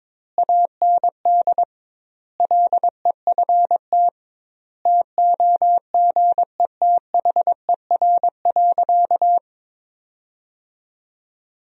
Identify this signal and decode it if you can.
Morse code, heard as a keyed tone, about 22 wpm, 700 Hz. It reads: AND LEFT TOGETHER.
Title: 2x Word-Spacing - Sentences from Top 400 Words - 22wpm